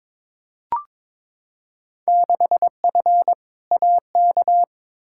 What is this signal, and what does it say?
Morse code, heard as a keyed tone, about 22 wpm, 700 Hz. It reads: 6F AK
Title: Field Day Exchanges Encore - 22wpm